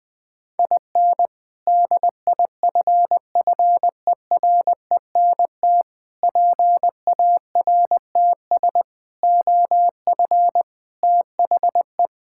Morse code, heard as a keyed tone, about 21 wpm, 700 Hz. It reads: IN DIFFERENT PARTS OF THE
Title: Sentences with 3-Word Phrases - 20wpm